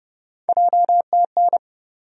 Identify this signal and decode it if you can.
Morse code, heard as a keyed tone, about 32 wpm, 700 Hz. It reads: JTN
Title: USA Call Sign Postfixes - 3 Letters 30wpm